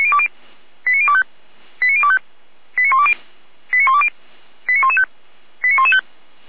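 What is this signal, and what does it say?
Five-tone selective call (ZVEI), heard as quick runs of five tones, tones of about 70 ms. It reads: 89025 89025 89120 89120 89115 89115
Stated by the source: ZVEI five-tone selective call received with an SDR.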